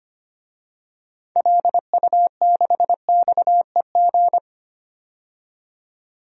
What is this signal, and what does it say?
Morse code, heard as a keyed tone, about 25 wpm, 700 Hz. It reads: LU6XEG
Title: DXCC-100 Call Signs - 6 Characters - 25wpm